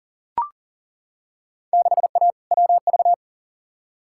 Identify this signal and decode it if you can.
Morse code, heard as a keyed tone, about 40 wpm, 700 Hz. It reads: 6A WV
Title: Field Day Exchanges - 40wpm